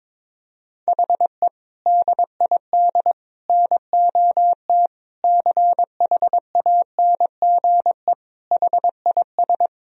Morse code, heard as a keyed tone, about 22 wpm, 700 Hz. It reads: HE DID NOT CHANGE HIS
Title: Sentences from Top 1000 Words - Review Part 1 - 22wpm